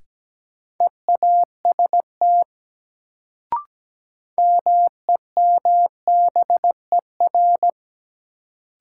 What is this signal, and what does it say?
Morse code, heard as a keyed tone, about 17 wpm, 700 Hz. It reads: EAST MEMBER
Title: Sets of 3 Words - New Word Drill - 17wpm